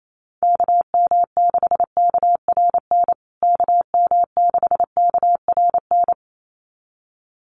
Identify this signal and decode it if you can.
Morse code, heard as a keyed tone, about 28 wpm, 700 Hz. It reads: KM6KRN KM6KRN
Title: USA Call Signs Repeated 2x 28wpm